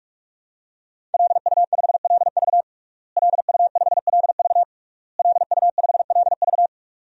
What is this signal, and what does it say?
Morse code, heard as a keyed tone, about 41 wpm, 700 Hz. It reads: LU5LV LU5LV LU5LV
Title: DXCC-100 Call Signs - 5 Characters - Repeated 3x - 45wpm